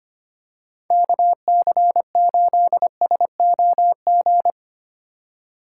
Morse code, heard as a keyed tone, about 25 wpm, 700 Hz. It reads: KC8SOG
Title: USA Call Signs - 6 Characters - 4x Character Spacing - 25wpm